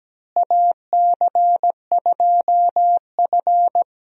Morse code, heard as a keyed tone, about 17 wpm, 700 Hz. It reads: AC2F
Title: USA Call Signs - 4 Characters - Repeated 3x - 17wpm